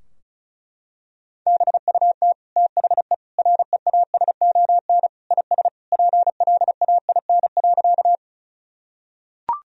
Morse code, heard as a keyed tone, about 36 wpm, 700 Hz. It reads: BUT THE REASON IS PLAIN.
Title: Sentences from Top 1000 Words - Review Part 2 - 35wpm